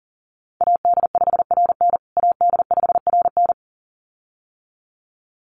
Morse code, heard as a keyed tone, about 40 wpm, 700 Hz. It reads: AD5RN AD5RN
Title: USA Call Signs Repeated 2x 40wpm